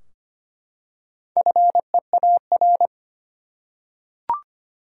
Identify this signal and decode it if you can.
Morse code, heard as a keyed tone, about 25 wpm, 700 Hz. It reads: FEAR
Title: Four Letter Words - 25wpm